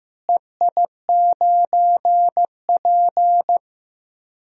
Morse code, heard as a keyed tone, about 15 wpm, 700 Hz. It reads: EI9P
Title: DXCC-100 Call Signs - 4 Characters - Repeated 3x - 15wpm